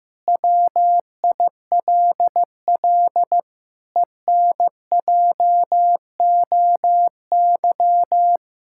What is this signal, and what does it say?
Morse code, heard as a keyed tone, about 15 wpm, 700 Hz. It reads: WILL ENJOY
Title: Sentences from Top 1000 Words - Review Part 5 - 15wpm